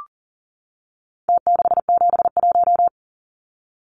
Morse code, heard as a keyed tone, about 40 wpm, 700 Hz. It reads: T671
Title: Numbers - Contest Style - 4-Digit 40wpm